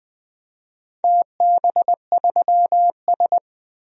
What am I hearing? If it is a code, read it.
Morse code, heard as a keyed tone, about 20 wpm, 700 Hz. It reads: TB3S
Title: DXCC-100 Call Signs - 4 Characters - 20wpm